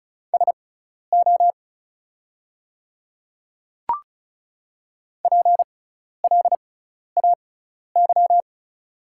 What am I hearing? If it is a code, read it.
Morse code, heard as a keyed tone, about 36 wpm, spaced as for 12 wpm, 700 Hz. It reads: SO PLAY
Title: 6x Character Spacing - Top 200 Words - 35wpm